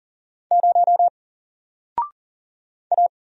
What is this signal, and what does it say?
Morse code, heard as a keyed tone, about 40 wpm, 700 Hz. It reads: 0 A
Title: Single Character (Letter:Number:Punctuation:Prosign) 40wpm